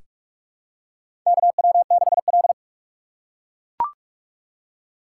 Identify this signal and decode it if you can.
Morse code, heard as a keyed tone, about 43 wpm, 700 Hz. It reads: KW6L
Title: USA Call Signs - 4 Characters - 3x Character Spacing - 45wpm